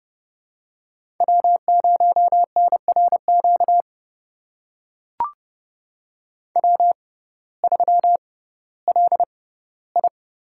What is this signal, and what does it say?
Morse code, heard as a keyed tone, about 30 wpm, spaced as for 17 wpm, 700 Hz. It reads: W0NRQ W3LI
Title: USA Call Signs - 6x Character Spacing - 30wpm